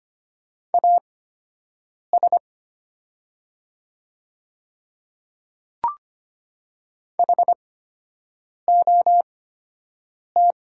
Morse code, heard as a keyed tone, about 25 wpm, spaced as for 7 wpm, 700 Hz. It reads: AS HOT
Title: 8x Character Spacing - Top 200 Words - 25wpm